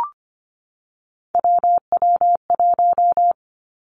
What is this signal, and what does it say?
Morse code, heard as a keyed tone, about 25 wpm, 700 Hz. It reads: WW1
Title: Three Challenging Letter-Number Combinations 25wpm